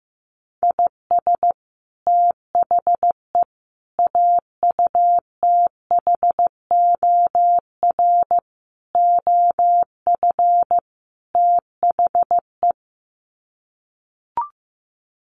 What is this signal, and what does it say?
Morse code, heard as a keyed tone, about 15 wpm, 700 Hz. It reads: IS THE AUTHOR OF THE
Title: Sets of 5 Words Encore 15wpm